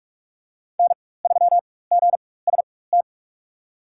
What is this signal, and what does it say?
Morse code, heard as a keyed tone, about 43 wpm, spaced as for 21 wpm, 700 Hz. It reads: N3GST
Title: USA Call Signs - 5 Characters - 4x Character Spacing - 45wpm